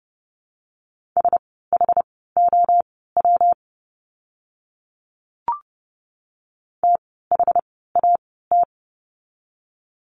Morse code, heard as a keyed tone, about 30 wpm, spaced as for 17 wpm, 700 Hz. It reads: SHOW THAT
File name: course / 30:10wpm / 115 Lesson 19f - Letter w - Words 30wpm